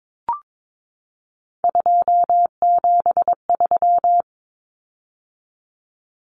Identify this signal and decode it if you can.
Morse code, heard as a keyed tone, about 22 wpm, 700 Hz. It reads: 273
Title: Common Cut Numbers - 3-Digit 22wpm